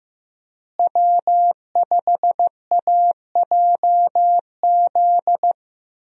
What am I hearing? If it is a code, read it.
Morse code, heard as a keyed tone, about 15 wpm, 700 Hz. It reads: W5AJZ
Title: USA Call Signs - 5 Characters - Repeated 3x - 15wpm